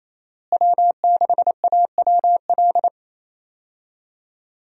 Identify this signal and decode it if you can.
Morse code, heard as a keyed tone, about 28 wpm, 700 Hz. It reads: W6AWL
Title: USA Call Signs - 5 Characters - Repeated 2x - 28wpm